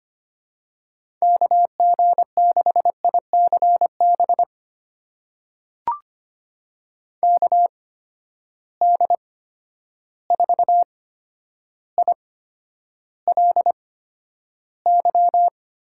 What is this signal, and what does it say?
Morse code, heard as a keyed tone, about 25 wpm, spaced as for 11 wpm, 700 Hz. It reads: KG6ICB KD4ILY
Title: USA Call Signs - 6 Characters - 8x Character Spacing - 25wpm